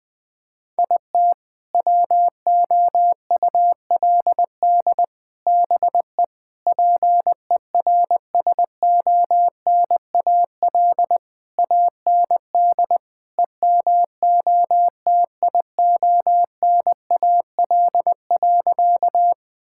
Morse code, heard as a keyed tone, about 20 wpm, 700 Hz. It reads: IT WOULD BE PERSONAL AND EMOTIONAL.
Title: Sentences from Top 1000 Words - Review Part 5 - 20wpm